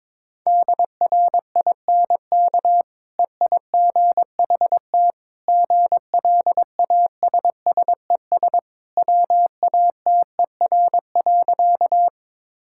Morse code, heard as a keyed tone, about 22 wpm, 700 Hz. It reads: DRINK EIGHT GLASSES WATER.
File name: practice / 22wpm / Easy Sentences with Hints - 22wpm